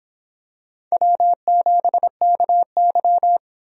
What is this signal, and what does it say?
Morse code, heard as a keyed tone, about 24 wpm, 700 Hz. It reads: W7KY